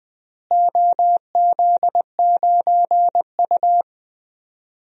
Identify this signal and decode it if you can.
Morse code, heard as a keyed tone, about 21 wpm, 700 Hz. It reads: OZ9U
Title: DXCC-100 Call Signs - 4 Characters - Repeated 3x - 20wpm